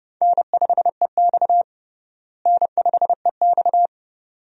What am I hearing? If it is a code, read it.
Morse code, heard as a keyed tone, about 30 wpm, 700 Hz. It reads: N5EX N5EX
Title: USA Call Signs - 4 Characters - Repeated 2x - 30wpm